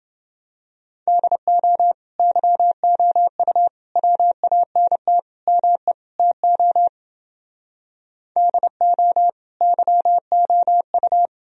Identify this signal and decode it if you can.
Morse code, heard as a keyed tone, about 29 wpm, 700 Hz. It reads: DO YOU WANT ME TO DO YOU